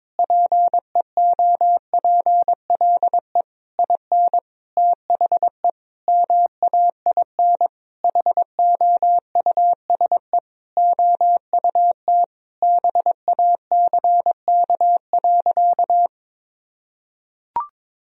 Morse code, heard as a keyed tone, about 22 wpm, 700 Hz. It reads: PEOPLE IN THE MAIN HOUSE OUT BACK.
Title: Sentences from Top 1000 Words - Review Part 2 - 22wpm